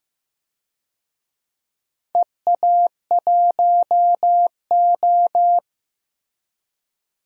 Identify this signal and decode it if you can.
Morse code, heard as a keyed tone, about 15 wpm, 700 Hz. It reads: EA1O
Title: DXCC-100 Call Signs - 4 Characters - Repeated 2x - 15wpm